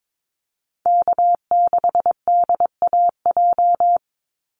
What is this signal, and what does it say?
Morse code, heard as a keyed tone, about 22 wpm, 700 Hz. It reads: K6DAJ
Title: Call Signs Repeated 2x - 22wpm